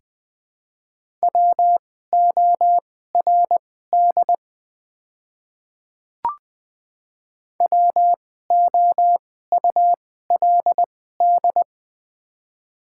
Morse code, heard as a keyed tone, about 20 wpm, spaced as for 14 wpm, 700 Hz. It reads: WORD WOULD
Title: 2x Character Spacing - Top 100 Words - 20wpm